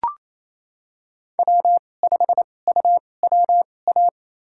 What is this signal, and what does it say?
Morse code, heard as a keyed tone, about 28 wpm, spaced as for 20 wpm, 700 Hz. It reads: W5UWA
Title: USA Call Signs - 5 Characters - 2x Character Spacing - 28wpm